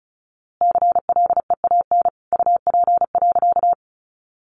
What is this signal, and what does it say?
Morse code, heard as a keyed tone, about 34 wpm, 700 Hz. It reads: CLEAN UP.
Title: A Sound Of Thunder 35wpm